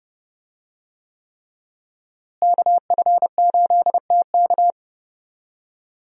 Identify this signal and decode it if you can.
Morse code, heard as a keyed tone, about 32 wpm, 700 Hz. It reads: KF8TK
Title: USA Call Signs - 5 Characters - Repeated 3x - 30wpm